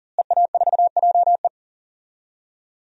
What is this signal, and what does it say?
Morse code, heard as a keyed tone, about 40 wpm, 700 Hz. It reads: EA4JE
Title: DXCC-100 Call Signs - 5 Characters - Repeated 2x - 40wpm